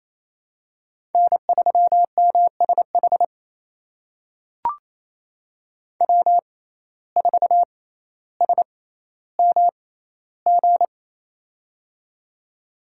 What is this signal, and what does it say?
Morse code, heard as a keyed tone, about 28 wpm, spaced as for 14 wpm, 700 Hz. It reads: N3MSH W4SMG
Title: USA Call Signs - 5 Characters - 6x Character Spacing - 28wpm